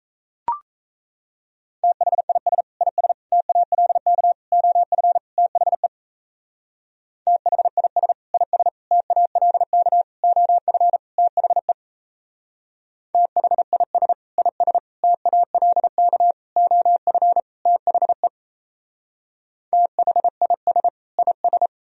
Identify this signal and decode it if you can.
Morse code, heard as a keyed tone, about 35 wpm, 700 Hz. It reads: THIS IS TALK OF THE THIS IS TALK OF THE THIS IS TALK OF THE THIS IS
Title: Speed-Racing - Sets of 5 Words 28wpm